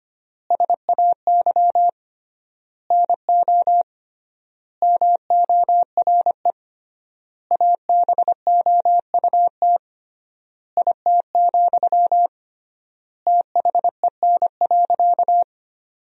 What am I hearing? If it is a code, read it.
Morse code, heard as a keyed tone, about 25 wpm, 700 Hz. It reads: SAY NO MORE ABOUT IT, THEN.
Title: 3x Word-Spacing - Sentences from Top 200 Words - 25wpm